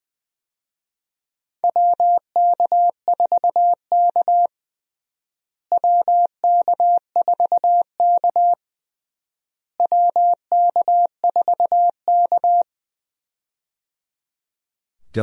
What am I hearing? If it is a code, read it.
Morse code, heard as a keyed tone, about 20 wpm, 700 Hz. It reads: WK4K WK4K WK4K
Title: USA Call Signs - 4 Characters - Repeated 3x - 20wpm